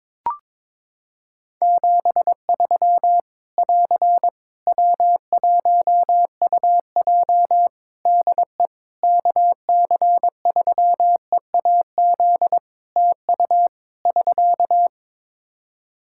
Morse code, heard as a keyed tone, about 22 wpm, 700 Hz. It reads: 73 <AR> W1UJ DE KC3EAZ TU <SK>
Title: Practice QSO - Exchange 4 - Ending - 22wpm